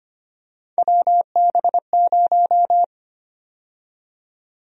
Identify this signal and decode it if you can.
Morse code, heard as a keyed tone, about 25 wpm, 700 Hz. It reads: WB0
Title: USA Call Sign Prefixes - 25wpm